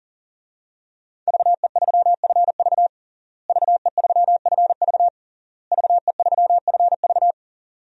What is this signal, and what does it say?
Morse code, heard as a keyed tone, about 40 wpm, 700 Hz. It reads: VE3FV VE3FV VE3FV
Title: DXCC-100 Call Signs - 5 Characters - Repeated 3x - 40wpm